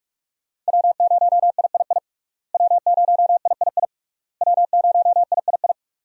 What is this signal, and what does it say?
Morse code, heard as a keyed tone, about 41 wpm, 700 Hz. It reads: W0III W0III W0III
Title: USA Call Signs - 5 Characters - Repeated 3x - 45wpm